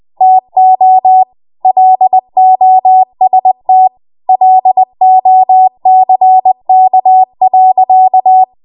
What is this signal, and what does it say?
Morse code, heard as a keyed tone, about 20 wpm, 750 Hz. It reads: TO LOST LOCK.
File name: Morse / 20WPM